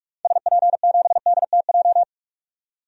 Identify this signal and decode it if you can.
Morse code, heard as a keyed tone, about 42 wpm, 700 Hz. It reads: SP7DTJ
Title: DXCC-100 Call Signs - 6 Characters - 45wpm